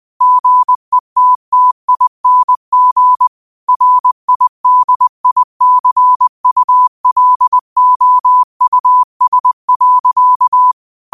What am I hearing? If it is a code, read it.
Morse code, heard as a keyed tone, about 20 wpm, 1000 Hz. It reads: GETTING RIDICULOUS.
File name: qotdfiles / 20wpm